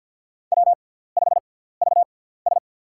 Morse code, heard as a keyed tone, about 53 wpm, spaced as for 17 wpm, 700 Hz. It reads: W5VS
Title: USA Call Signs - 6x Character Spacing - 50wpm